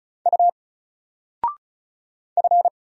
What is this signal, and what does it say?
Morse code, heard as a keyed tone, about 36 wpm, 700 Hz. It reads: U F
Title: Single Character (Letter:Number:Punctuation:Prosign) 35wpm